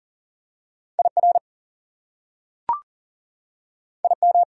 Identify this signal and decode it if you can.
Morse code, heard as a keyed tone, about 39 wpm, 700 Hz. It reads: IR IM
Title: Prefixes and Suffixes 40wpm